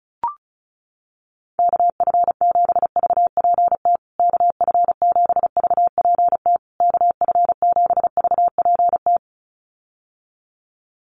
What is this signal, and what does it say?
Morse code, heard as a keyed tone, about 34 wpm, 700 Hz. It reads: KF7VPT KF7VPT KF7VPT
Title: USA Call Signs Repeated 3x 35wpm